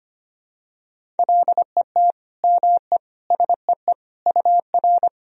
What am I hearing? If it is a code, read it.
Morse code, heard as a keyed tone, about 25 wpm, 700 Hz. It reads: LET ME SEE UR
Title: Sentences from Top 500 Words and Abbreviations 25wpm